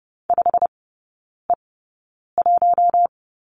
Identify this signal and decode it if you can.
Morse code, heard as a keyed tone, about 30 wpm, spaced as for 9 wpm, 700 Hz. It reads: 5E1
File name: course / 30:10wpm / 087 Lesson 15e - Number 5 - Three Characters 30wpm